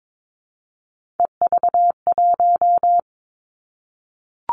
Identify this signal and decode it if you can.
Morse code, heard as a keyed tone, about 22 wpm, 700 Hz. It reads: EV1
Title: Three Challenging Letter-Number Combinations 22wpm